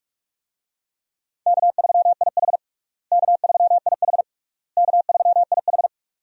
Morse code, heard as a keyed tone, about 41 wpm, 700 Hz. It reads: K3IH K3IH K3IH
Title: USA Call Signs - 4 Characters - Repeated 3x - 45wpm